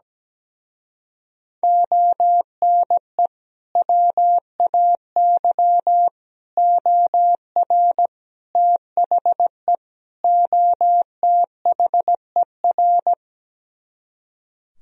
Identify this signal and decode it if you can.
Morse code, heard as a keyed tone, about 17 wpm, 700 Hz. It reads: ONE WAY OR THE OTHER
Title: Speed-Racing - Sets of 5 Words 17wpm